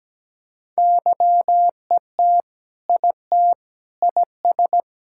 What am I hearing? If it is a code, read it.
Morse code, heard as a keyed tone, about 17 wpm, 700 Hz. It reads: YET IT IS
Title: Sentences from Top 1600 Words 17wpm